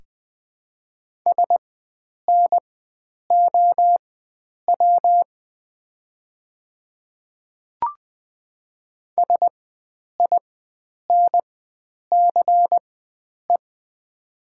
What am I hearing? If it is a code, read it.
Morse code, heard as a keyed tone, about 20 wpm, spaced as for 9 wpm, 700 Hz. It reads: SNOW SINCE
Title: 4x Character Spacing - Top 500 Words - 20wpm